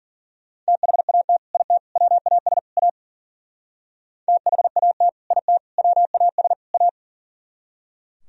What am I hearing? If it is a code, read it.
Morse code, heard as a keyed tone, about 50 wpm, 700 Hz. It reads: THAT IT WAS A THAT IT WAS A